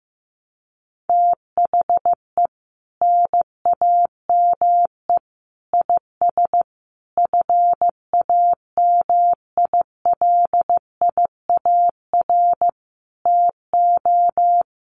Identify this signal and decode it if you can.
Morse code, heard as a keyed tone, about 14 wpm, 700 Hz. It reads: THE NAME IS FAMILIAR TO
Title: Sentences from Top 1000 Words 15wpm